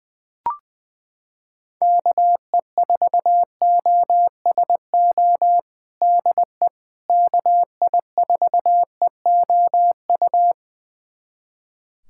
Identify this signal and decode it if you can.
Morse code, heard as a keyed tone, about 20 wpm, 700 Hz. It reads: KE4OSO DE KI4EOU
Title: USA Callsign DE Callsign - 6 Characters - 20wpm